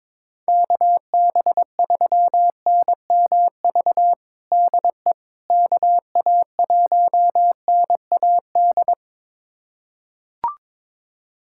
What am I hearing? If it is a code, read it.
Morse code, heard as a keyed tone, about 22 wpm, 700 Hz. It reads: KB3NMV DE KA1NAD
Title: USA Callsign DE Callsign - 6 Characters - 22wpm